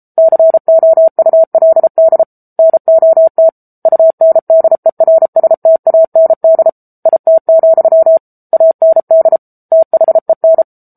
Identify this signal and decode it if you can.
Morse code, heard as a keyed tone, about 34 wpm, 650 Hz. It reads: COULD NOT UNDERSTAND IT, AND THEN
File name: Book Two / 07-Chapter 7 - 33 WPM